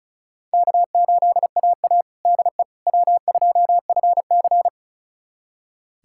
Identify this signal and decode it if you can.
Morse code, heard as a keyed tone, about 36 wpm, 700 Hz. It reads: K8AA DE W2FC
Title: USA Callsign DE Callsign - 4 Characters - 35wpm